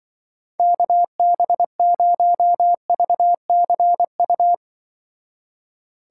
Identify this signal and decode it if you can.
Morse code, heard as a keyed tone, about 23 wpm, 700 Hz. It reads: KB0VCU
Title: Speed Racing - USA 2x3 Call Signs 20wpm